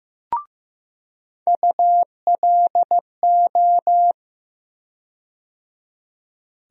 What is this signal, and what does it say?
Morse code, heard as a keyed tone, about 15 wpm, 700 Hz. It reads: ULO
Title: USA Call Sign Postfixes - 3 Letters 15wpm